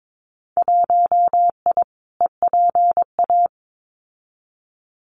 Morse code, heard as a keyed tone, about 22 wpm, 700 Hz. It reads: 1I EPA
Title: Winter Field Day Exchanges Encore 22wpm